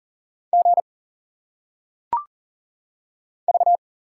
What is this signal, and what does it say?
Morse code, heard as a keyed tone, about 40 wpm, 700 Hz. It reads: G V
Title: Single Letters - Phonetic Pronunciation 40wpm